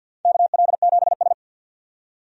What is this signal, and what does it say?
Morse code, heard as a keyed tone, about 53 wpm, 700 Hz. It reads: KL7S
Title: DXCC-100 Call Signs - 4 Characters - 50wpm